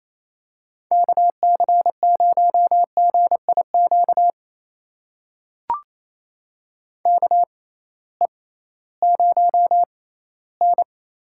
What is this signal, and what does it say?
Morse code, heard as a keyed tone, about 28 wpm, spaced as for 16 wpm, 700 Hz. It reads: KC0GIQ KE0N
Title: USA Call Signs - 6 Characters - 6x Character Spacing - 28wpm